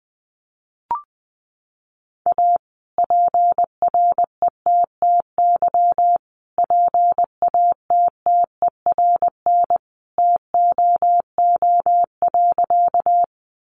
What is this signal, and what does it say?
Morse code, heard as a keyed tone, about 20 wpm, 700 Hz. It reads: A PRETTY PATTERN TOO.
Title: Sentences from Top 1000 Words 20wpm